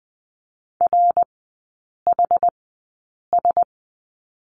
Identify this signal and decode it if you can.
Morse code, heard as a keyed tone, about 20 wpm, spaced as for 8 wpm, 700 Hz. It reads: RHS
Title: 178 Lesson 29e - Slash - Three Characters 20wpm